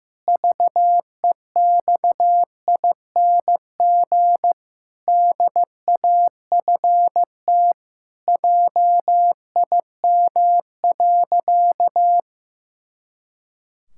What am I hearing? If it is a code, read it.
Morse code, heard as a keyed tone, about 16 wpm, 700 Hz. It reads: VEXING DAFT JIM.
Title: Pangram Challenge v2 - 15wpm